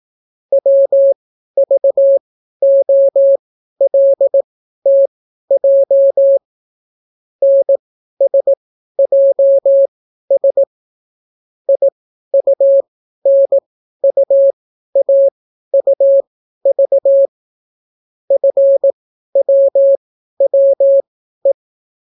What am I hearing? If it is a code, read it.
Morse code, heard as a keyed tone, about 18 wpm, spaced as for 12 wpm, 550 Hz. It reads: WVOLTJ NSJS IUNUAUV FWWE